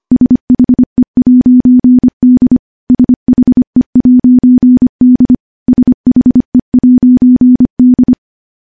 Morse code, heard as a keyed tone, about 25 wpm, 250 Hz. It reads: SHE'D SHE'D SHE'D
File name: morse-training-25WPM